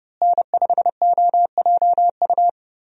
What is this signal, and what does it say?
Morse code, heard as a keyed tone, about 31 wpm, 700 Hz. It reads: N5OJU
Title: USA Call Signs - 5 Characters - Repeated 2x - 30wpm